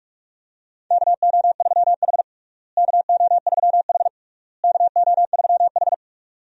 Morse code, heard as a keyed tone, about 46 wpm, 700 Hz. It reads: KO3H KO3H KO3H
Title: USA Call Signs - 4 Characters - Repeated 3x - 45wpm